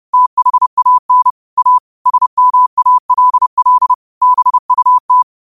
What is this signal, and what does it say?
Morse code, heard as a keyed tone, about 30 wpm, 1000 Hz. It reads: THAN A SMALL BUT